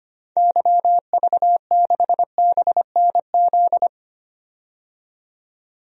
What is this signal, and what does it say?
Morse code, heard as a keyed tone, about 25 wpm, 700 Hz. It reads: YV6BNZ
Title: DXCC-100 Call Signs - 6 Characters - 25wpm